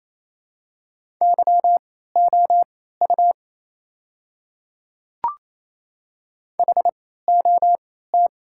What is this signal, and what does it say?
Morse code, heard as a keyed tone, about 28 wpm, spaced as for 16 wpm, 700 Hz. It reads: YOU HOT
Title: 3x Character Spacing - Top 100 Words - 28wpm